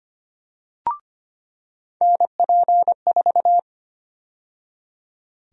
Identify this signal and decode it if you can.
Morse code, heard as a keyed tone, about 25 wpm, 700 Hz. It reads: NP4
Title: USA Call Sign Prefixes - 25wpm